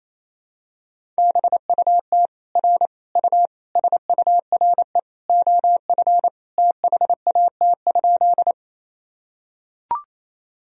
Morse code, heard as a keyed tone, about 28 wpm, 700 Hz. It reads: BUT R U SURE OF THAT?
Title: Sentences from Top 300 Words and Abbreviations 28wpm